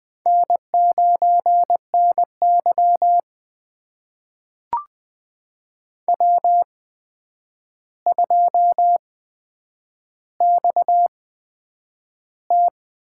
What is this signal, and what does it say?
Morse code, heard as a keyed tone, about 20 wpm, spaced as for 9 wpm, 700 Hz. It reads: N9NY W2XT
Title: USA Call Signs - 4 Characters - 8x Character Spacing - 20wpm